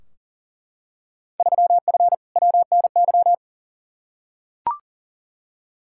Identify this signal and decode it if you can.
Morse code, heard as a keyed tone, about 40 wpm, 700 Hz. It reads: 3F WNY
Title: Field Day Exchanges - 40wpm